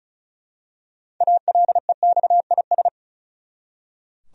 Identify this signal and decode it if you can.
Morse code, heard as a keyed tone, about 34 wpm, 700 Hz. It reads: ALEXIS
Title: USA Common Names - 35wpm